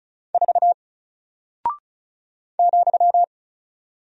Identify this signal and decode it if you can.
Morse code, heard as a keyed tone, about 36 wpm, 700 Hz. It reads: 4 ,
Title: Single Character (Letter:Number:Punctuation:Prosign) Alt 35wpm